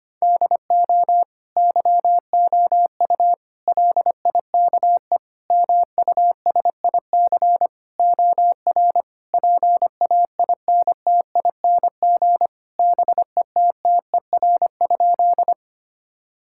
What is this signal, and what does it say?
Morse code, heard as a keyed tone, about 25 wpm, 700 Hz. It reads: DO YOU LIKE MUSIC OR PAINTING BETTER?
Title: Sentences from Top 1000 Words - Review Part 6 - 25wpm